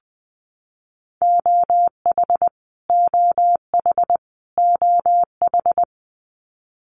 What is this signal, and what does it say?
Morse code, heard as a keyed tone, about 20 wpm, 700 Hz. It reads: OH OH OH
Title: Top 500 Words Repeated 20wpm